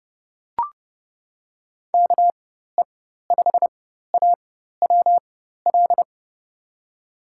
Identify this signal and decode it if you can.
Morse code, heard as a keyed tone, about 30 wpm, spaced as for 14 wpm, 700 Hz. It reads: KE5AWL
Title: USA Call Signs - 6 Characters - 4x Character Spacing - 30wpm